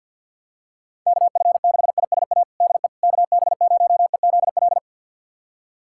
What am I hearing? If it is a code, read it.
Morse code, heard as a keyed tone, about 52 wpm, 700 Hz. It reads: KF6ISA DE KB0EZL
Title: USA Callsign DE Callsign - 6 Characters - 50wpm